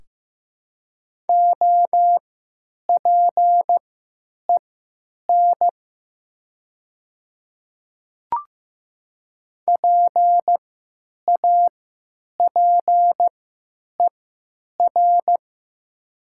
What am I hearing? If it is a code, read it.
Morse code, heard as a keyed tone, about 15 wpm, spaced as for 9 wpm, 700 Hz. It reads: OPEN PAPER
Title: 3x Character Spacing - Top 500 Words - 15wpm